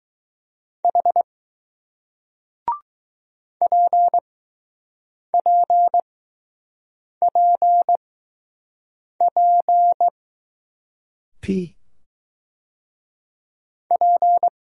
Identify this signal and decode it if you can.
Morse code, heard as a keyed tone, about 20 wpm, spaced as for 6 wpm, 700 Hz. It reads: H PPPP P